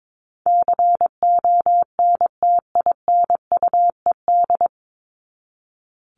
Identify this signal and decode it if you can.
Morse code, heard as a keyed tone, about 22 wpm, 700 Hz. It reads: CONTINUED
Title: ED Suffix - Top 100 Words with Base Words 22wpm